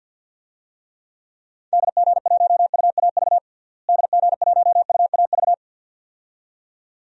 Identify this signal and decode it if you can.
Morse code, heard as a keyed tone, about 52 wpm, 700 Hz. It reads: DG1UAV DG1UAV
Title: DXCC-100 Call Signs - 6 Characters - Repeated 2x - 50wpm